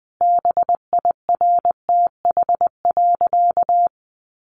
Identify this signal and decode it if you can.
Morse code, heard as a keyed tone, about 20 wpm, 700 Hz. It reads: BIRTH.
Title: Sentences from Top 900 Words 20wpm